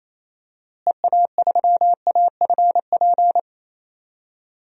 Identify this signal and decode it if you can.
Morse code, heard as a keyed tone, about 28 wpm, 700 Hz. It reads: EA3AFP
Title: DXCC-100 Call Signs - 6 Characters - Repeated 2x - 28wpm